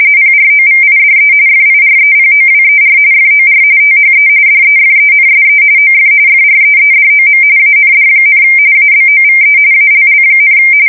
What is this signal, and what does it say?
RTTY, about 45.45 baud. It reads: PROVIDE FIREPOWER FOR THE DEFENSE OF FREEDOM AND, ABOVE ALL,
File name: fsk170 / battleshipnj